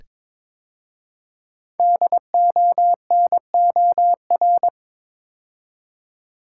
Morse code, heard as a keyed tone, about 22 wpm, 700 Hz. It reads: DONOR
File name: practice / 22wpm / 5 Letter Words - First 3 Letters Given - 22wpm